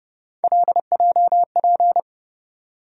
Morse code, heard as a keyed tone, about 30 wpm, 700 Hz. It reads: LJP
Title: USA Call Sign Postfixes - 3 Letters 30wpm